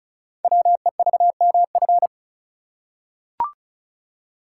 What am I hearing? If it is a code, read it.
Morse code, heard as a keyed tone, about 36 wpm, 700 Hz. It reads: WEVMF
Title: Code Groups - 5 Letters - 35wpm